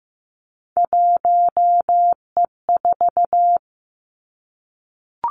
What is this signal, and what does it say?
Morse code, heard as a keyed tone, about 15 wpm, 700 Hz. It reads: 1E4
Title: Three Challenging Letter-Number Combinations 15wpm